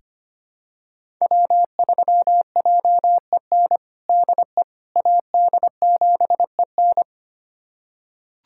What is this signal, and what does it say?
Morse code, heard as a keyed tone, about 25 wpm, 700 Hz. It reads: W3JEN DE AD7EN
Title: USA Callsign DE Callsign - 5 Characters - 25wpm